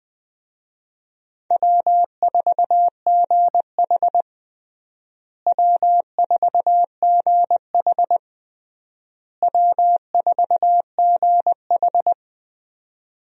Morse code, heard as a keyed tone, about 20 wpm, 700 Hz. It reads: W4GH W4GH W4GH
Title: USA Call Signs - 4 Characters - Repeated 3x - 20wpm